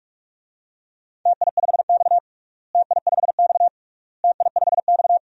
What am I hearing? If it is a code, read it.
Morse code, heard as a keyed tone, about 46 wpm, 700 Hz. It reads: TI5X TI5X TI5X
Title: DXCC-100 Call Signs - 4 Characters - Repeated 3x - 45wpm